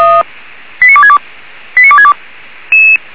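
Five-tone selective call (ZVEI), heard as quick runs of five tones, tones of about 70 ms. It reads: 89262 89262
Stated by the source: ZVEI five-tone selective call received on an SDR.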